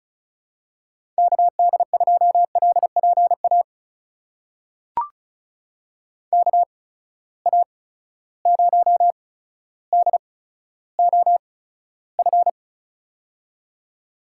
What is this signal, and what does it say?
Morse code, heard as a keyed tone, about 36 wpm, spaced as for 15 wpm, 700 Hz. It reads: KD2LPA KA0DOF
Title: USA Call Signs - 6 Characters - 8x Character Spacing - 35wpm